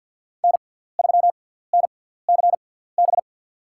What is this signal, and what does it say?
Morse code, heard as a keyed tone, about 53 wpm, spaced as for 17 wpm, 700 Hz. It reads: N3NCB
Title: USA Call Signs - 5 Characters - 6x Character Spacing - 50wpm